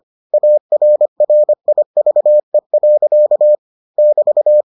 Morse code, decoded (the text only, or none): ARRIVE. <BT>